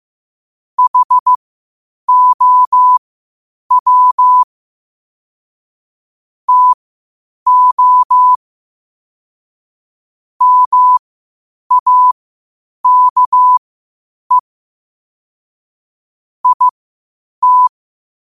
Morse code, heard as a keyed tone, about 15 wpm, spaced as for 8 wpm, 1000 Hz. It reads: HOW TO MAKE IT
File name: qotdfiles / 8wpm